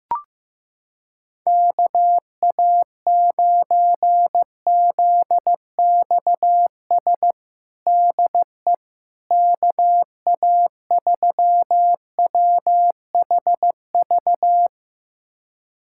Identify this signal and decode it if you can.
Morse code, heard as a keyed tone, about 16 wpm, 700 Hz. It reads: KA9ZXS DE KA3WHV
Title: USA Callsign DE Callsign - 6 Characters - 15wpm